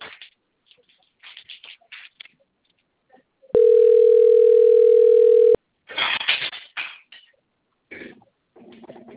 Ringback tone (North America), 440 + 480 Hz.